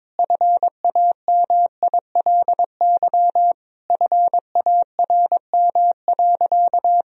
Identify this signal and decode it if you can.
Morse code, heard as a keyed tone, about 22 wpm, 700 Hz. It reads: FAMILY FARM.
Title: Sentences from Top 1000 Words - Review Part 2 - 22wpm